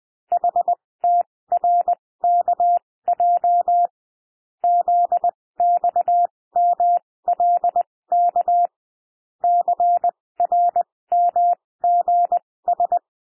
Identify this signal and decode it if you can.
Morse code, heard as a keyed tone, about 20 wpm, spaced as for 16 wpm, 700 Hz. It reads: HTRKJ ZXMLK CRMGS
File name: cwmp3 / L1 SOUND LETTERS 20 wpm